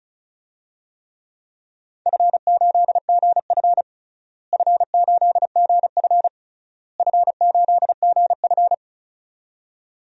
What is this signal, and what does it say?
Morse code, heard as a keyed tone, about 36 wpm, 700 Hz. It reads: F8GF F8GF F8GF
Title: DXCC-100 Call Signs - 4 Characters - Repeated 3x - 35wpm